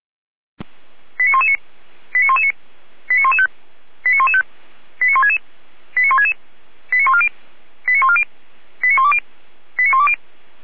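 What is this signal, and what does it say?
Five-tone selective call (ZVEI), heard as quick runs of five tones, tones of about 80 ms. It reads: 89119 89119 89115 89115 89166 89166 89140 89140 89120 89120
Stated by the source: ZVEI five-tone selective call received by SDR.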